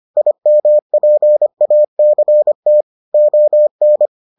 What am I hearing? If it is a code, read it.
Morse code, heard as a keyed tone, about 25 wpm, 600 Hz. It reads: IMPACT ON